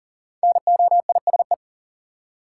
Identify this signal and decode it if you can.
Morse code, heard as a keyed tone, about 40 wpm, 700 Hz. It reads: NOISE
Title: Five Letter Words - 40wpm